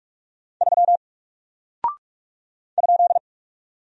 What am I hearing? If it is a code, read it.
Morse code, heard as a keyed tone, about 43 wpm, 700 Hz. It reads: 3 ?
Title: Single Character (Letter:Number:Punctuation:Prosign) Alt 45wpm